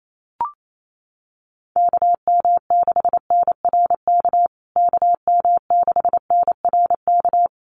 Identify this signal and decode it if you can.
Morse code, heard as a keyed tone, about 28 wpm, 700 Hz. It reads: KM6NRK KM6NRK
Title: USA Call Signs Repeated 2x 28wpm